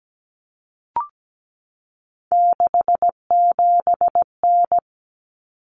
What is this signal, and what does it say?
Morse code, heard as a keyed tone, about 17 wpm, 700 Hz. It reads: 67N
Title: Common Cut Numbers - 3-Digit 17wpm